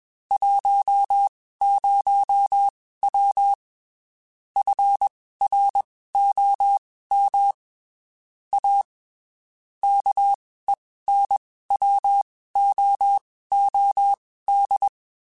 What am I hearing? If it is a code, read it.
Morse code, heard as a keyed tone, about 21 wpm, spaced as for 15 wpm, 800 Hz. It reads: 10W FROM A KENWOOD
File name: (spaced) / QSOs Spaced-mp3s / WA7QOX 2 @20 sp